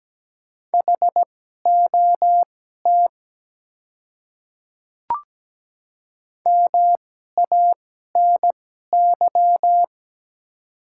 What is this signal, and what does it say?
Morse code, heard as a keyed tone, about 17 wpm, spaced as for 12 wpm, 690 Hz. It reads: HOT MANY
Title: 2x Character Spacing - Top 300 Words - 17wpm